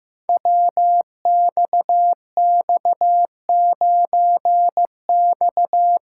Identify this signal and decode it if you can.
Morse code, heard as a keyed tone, about 15 wpm, 700 Hz. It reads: WXX9X
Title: Code Groups - 5 Letter:Numbers - 15wpm